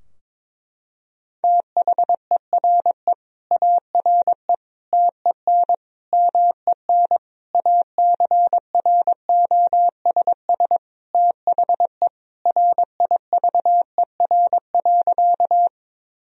Morse code, heard as a keyed tone, about 22 wpm, 700 Hz. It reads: THERE ARE TEN MEN ACROSS THE RIVER.
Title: Sentences from Top 1000 Words - Review Part 2 - 22wpm